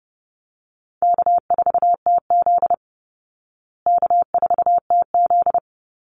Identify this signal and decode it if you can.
Morse code, heard as a keyed tone, about 30 wpm, 700 Hz. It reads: K4TZ K4TZ
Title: Call Signs Repeated 3x - 30wpm